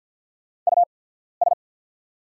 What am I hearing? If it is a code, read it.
Morse code, heard as a keyed tone, about 48 wpm, spaced as for 14 wpm, 690 Hz. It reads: US
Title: 8x Character Spacing - Top 200 Words - 50wpm